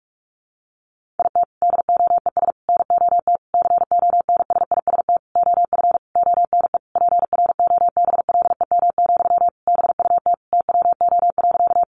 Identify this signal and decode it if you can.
Morse code, heard as a keyed tone, about 44 wpm, 700 Hz. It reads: IT DOES NOT CONSIST OF ONE PROBLEM, BUT TWO.
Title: Sentences from Top 1400 Words 45wpm